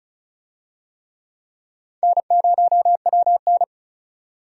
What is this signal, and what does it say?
Morse code, heard as a keyed tone, about 36 wpm, 700 Hz. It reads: N0WN
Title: USA Call Signs - 4 Characters - 35wpm